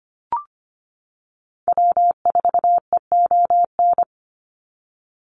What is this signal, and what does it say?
Morse code, heard as a keyed tone, about 25 wpm, 690 Hz. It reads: W4EON
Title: NA SOTA Activator Callsigns 25wpm